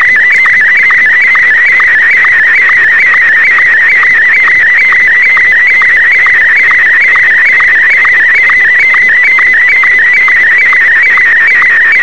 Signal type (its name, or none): SSTV (slow-scan television)